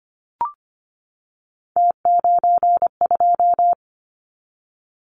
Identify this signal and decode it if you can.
Morse code, heard as a keyed tone, about 25 wpm, 700 Hz. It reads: T92
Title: Callsign Prefixes for Top 100 DXCC - 25wpm